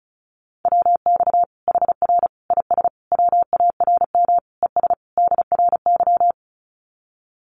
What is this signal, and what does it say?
Morse code, heard as a keyed tone, about 34 wpm, 700 Hz. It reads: WX HR IS WARM ES DRY
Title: Common QSO Phrases 35wpm